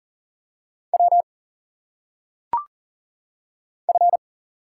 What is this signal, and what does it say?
Morse code, heard as a keyed tone, about 40 wpm, 700 Hz. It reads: W F
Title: Single Letters - Phonetic Pronunciation 40wpm